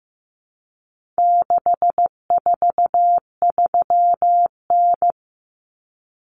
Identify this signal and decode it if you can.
Morse code, heard as a keyed tone, about 15 wpm, 700 Hz. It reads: 643N
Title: Common Cut Numbers - 4-Digit 15wpm